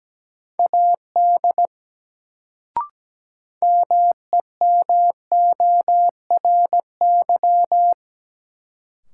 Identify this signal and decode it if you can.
Morse code, heard as a keyed tone, about 17 wpm, 700 Hz. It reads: AD MEMORY
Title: Sentences from Top 1300 Words - New Word Drill - 17wpm